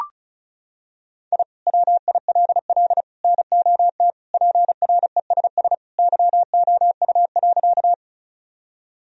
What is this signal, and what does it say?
Morse code, heard as a keyed tone, about 36 wpm, 700 Hz. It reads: I WILL NOT PRESS YOU.
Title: Sentences from Top 1000 Words - Review Part 1 - 35wpm